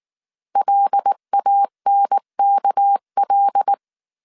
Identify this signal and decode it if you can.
Morse code, heard as a keyed tone, about 19 wpm, 800 Hz. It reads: LANKL
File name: CW / DET / 1 9 1500C S B 18wpm 2551